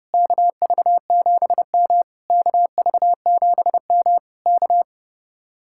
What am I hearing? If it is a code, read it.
Morse code, heard as a keyed tone, about 34 wpm, 700 Hz. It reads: KV7M KV7M K
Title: Speed-Racing Common QSO Phrases - 30wpm